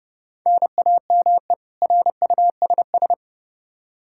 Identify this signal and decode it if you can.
Morse code, heard as a keyed tone, about 32 wpm, 700 Hz. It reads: NAME RUSS
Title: K1USN SST QSO Practice 30wpm